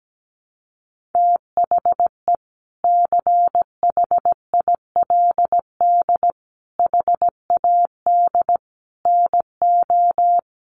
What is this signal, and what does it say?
Morse code, heard as a keyed tone, about 17 wpm, 700 Hz. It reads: THE CHILD HAD NO
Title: Sentences from Top 1400 Words 17wpm